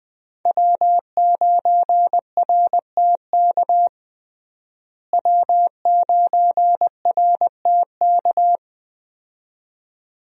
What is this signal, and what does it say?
Morse code, heard as a keyed tone, about 20 wpm, 700 Hz. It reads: W9RTK W9RTK
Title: USA Call Signs - 5 Characters - Repeated 2x - 20wpm